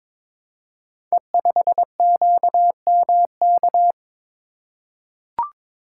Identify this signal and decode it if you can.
Morse code, heard as a keyed tone, about 22 wpm, 700 Hz. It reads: E5QMK
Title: Code Groups - 5 Letter:Numbers - 22wpm